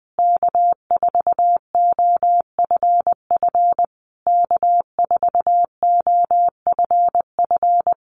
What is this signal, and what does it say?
Morse code, heard as a keyed tone, about 20 wpm, 700 Hz. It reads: K4OFF K4OFF
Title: Common QSO Phrases Encore 20wpm